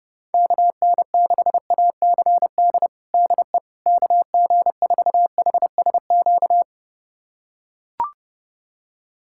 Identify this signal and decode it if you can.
Morse code, heard as a keyed tone, about 30 wpm, 700 Hz. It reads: KN6ACD DE KG4HSQ
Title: USA Callsign DE Callsign 30wpm